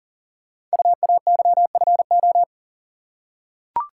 Morse code, heard as a keyed tone, about 43 wpm, 690 Hz. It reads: UAYFO
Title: Code Groups - 5 Letters - 40wpm